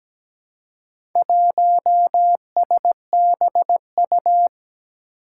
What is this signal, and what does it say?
Morse code, heard as a keyed tone, about 17 wpm, 700 Hz. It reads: 1SBU
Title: Four Challenging Letter-Number Combinations 17wpm